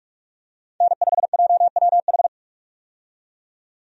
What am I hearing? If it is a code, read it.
Morse code, heard as a keyed tone, about 46 wpm, 700 Hz. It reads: N5JWH
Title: USA Call Signs - 5 Characters - 45wpm